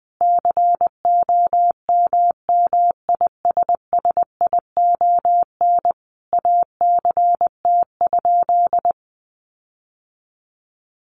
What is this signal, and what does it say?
Morse code, heard as a keyed tone, about 20 wpm, 700 Hz. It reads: COMMISSION ACT?
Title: Sentences from Top 1400 Words 20wpm